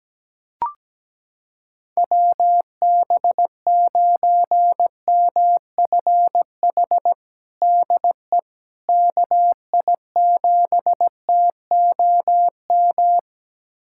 Morse code, heard as a keyed tone, about 17 wpm, 700 Hz. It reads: WB9MFH DE KI7TOM
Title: USA Callsign DE Callsign - 6 Characters - 17wpm